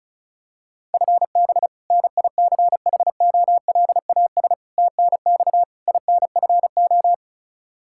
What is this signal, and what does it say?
Morse code, heard as a keyed tone, about 34 wpm, 700 Hz. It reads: FB NICHOLAS TNX INFO
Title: K1USN SST QSO Practice 35wpm